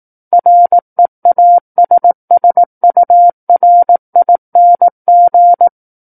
Morse code, heard as a keyed tone, about 18 wpm, 700 Hz. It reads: REASSURING
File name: 18 WPM